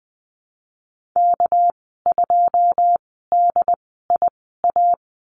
Morse code, heard as a keyed tone, about 20 wpm, spaced as for 15 wpm, 700 Hz. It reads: K2DIA
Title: 213 Lesson 34g - Number 8 - Call Signs 20wpm